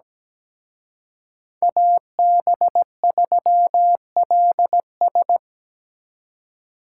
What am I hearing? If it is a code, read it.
Morse code, heard as a keyed tone, about 20 wpm, 700 Hz. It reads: AB3LS
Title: Speed Racing - USA 2x2 Call Signs 17wpm